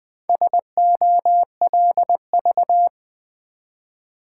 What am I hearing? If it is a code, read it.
Morse code, heard as a keyed tone, about 20 wpm, 700 Hz. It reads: SOLV
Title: Build A Word - 1 Letter At A Time - 3 to 5-Letter Words - 20wpm